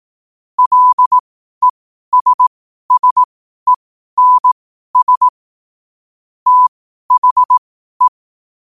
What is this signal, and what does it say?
Morse code, heard as a keyed tone, about 18 wpm, spaced as for 12 wpm, 1000 Hz. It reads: LESSENS THE
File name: qotdfiles / 12wpm